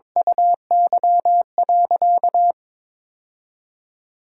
Morse code, heard as a keyed tone, about 22 wpm, 700 Hz. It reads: UY.
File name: practice / 22wpm / Sentences with 8-Word Phrases - 22wpm